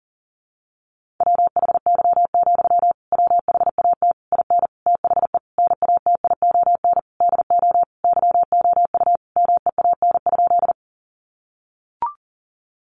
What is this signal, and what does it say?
Morse code, heard as a keyed tone, about 40 wpm, 700 Hz. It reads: WHY, WHAT IN THE NATION DO YOU MEAN?
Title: Sentences from Top 900 Words 40wpm